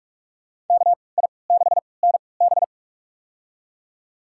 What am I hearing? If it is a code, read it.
Morse code, heard as a keyed tone, about 46 wpm, spaced as for 26 wpm, 700 Hz. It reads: KI6NB
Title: USA Call Signs - 5 Characters - 3x Character Spacing - 45wpm